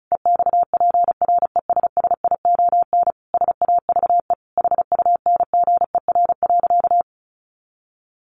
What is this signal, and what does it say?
Morse code, heard as a keyed tone, about 34 wpm, 700 Hz. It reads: EXPRESSION SAVE HUNGER.
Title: A Sound Of Thunder 35wpm